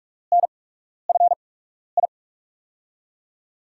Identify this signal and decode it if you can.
Morse code, heard as a keyed tone, about 46 wpm, spaced as for 12 wpm, 700 Hz. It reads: NFI
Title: USA Call Signs - 5 Characters - 8x Character Spacing - 45wpm